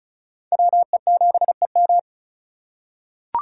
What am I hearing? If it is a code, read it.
Morse code, heard as a keyed tone, about 36 wpm, 700 Hz. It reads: WE7EM
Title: USA Call Signs - 5 Characters - 2x Character Spacing - 35wpm